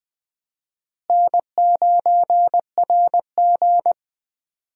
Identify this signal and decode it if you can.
Morse code, heard as a keyed tone, about 20 wpm, 700 Hz. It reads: N9RG
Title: USA Call Signs - 4 Characters - Repeated 3x - 20wpm